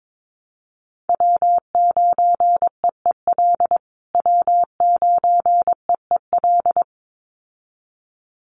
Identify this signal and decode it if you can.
Morse code, heard as a keyed tone, about 22 wpm, 700 Hz. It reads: W9EEL W9EEL
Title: USA Call Signs Repeated 2x 22wpm